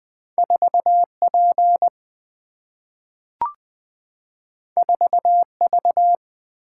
Morse code, heard as a keyed tone, about 20 wpm, 700 Hz. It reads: 4P 4V
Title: Single-Letter Single-Number 20wpm